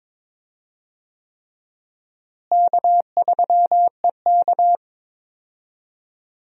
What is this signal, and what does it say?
Morse code, heard as a keyed tone, about 22 wpm, 700 Hz. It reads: K3EK
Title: USA Call Signs - 4 Characters - Repeated 3x - 22wpm